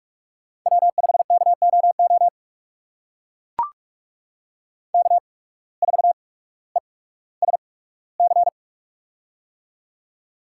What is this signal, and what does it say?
Morse code, heard as a keyed tone, about 44 wpm, spaced as for 19 wpm, 700 Hz. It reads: W5KOO K4ESC
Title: USA Call Signs - 5 Characters - 8x Character Spacing - 45wpm